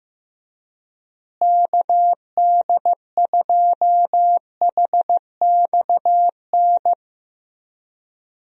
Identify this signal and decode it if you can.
Morse code, heard as a keyed tone, about 16 wpm, 700 Hz. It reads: KD2HXN
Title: USA Call Signs - 6 Characters - Repeated 3x - 15wpm